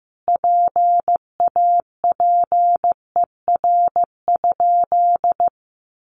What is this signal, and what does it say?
Morse code, heard as a keyed tone, about 15 wpm, 700 Hz. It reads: PAPER?
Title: Sentences from Top 1300 Words 15wpm